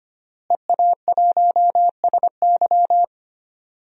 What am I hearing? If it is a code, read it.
Morse code, heard as a keyed tone, about 25 wpm, 700 Hz. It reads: EA1SY
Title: DXCC-100 Call Signs - 5 Characters - Repeated 3x - 25wpm